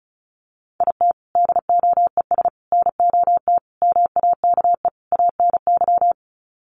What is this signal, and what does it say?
Morse code, heard as a keyed tone, about 34 wpm, 700 Hz. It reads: IT DOES NOT MAKE ANY
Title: Sets of 5 Words Encore2 35wpm